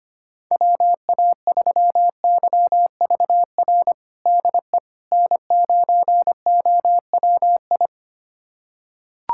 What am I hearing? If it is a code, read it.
Morse code, heard as a keyed tone, about 25 wpm, 700 Hz. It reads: WA3YVR DE N9OWI
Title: USA Callsign DE Callsign 25wpm